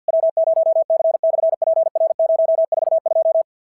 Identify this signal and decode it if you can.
Morse code, heard as a keyed tone, about 51 wpm, 650 Hz. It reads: W0C/PR042
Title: SOTA QSO Practice 50wpm